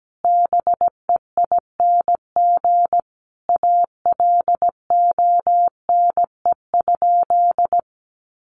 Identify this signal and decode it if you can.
Morse code, heard as a keyed tone, about 17 wpm, 700 Hz. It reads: BEING ALONE?
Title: Sentences from Top 1100 Words 17wpm